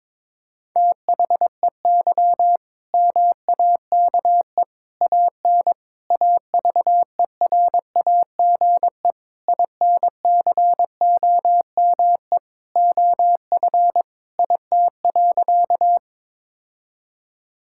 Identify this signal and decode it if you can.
Morse code, heard as a keyed tone, about 21 wpm, 700 Hz. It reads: THEY MAKE AN AVERAGE INCOME OF IT.
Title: Sentences from Top 1000 Words - Review Part 6 - 22wpm